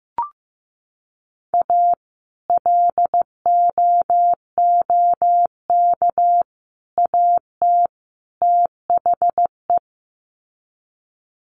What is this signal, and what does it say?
Morse code, heard as a keyed tone, about 15 wpm, 700 Hz. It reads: A LOOK AT THE
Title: Sets of 4 Words 15wpm